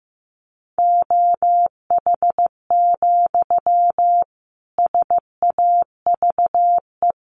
Morse code, heard as a keyed tone, about 15 wpm, 700 Hz. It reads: OH, SAVE
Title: Sentences from Top 1100 Words 15wpm